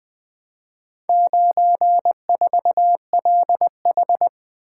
Morse code, heard as a keyed tone, about 20 wpm, 700 Hz. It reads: 94LH
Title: Four Letter-Number Combinations - 20wpm